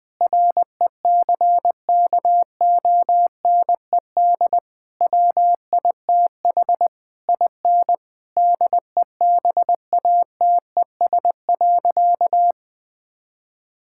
Morse code, heard as a keyed tone, about 20 wpm, 700 Hz. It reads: RECKONED WITH IN DEBATES.
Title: Sentences with 6-Word Phrases - Spoken Beginning - 20wpm